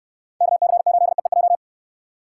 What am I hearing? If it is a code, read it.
Morse code, heard as a keyed tone, about 67 wpm, 700 Hz. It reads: KC8EEJ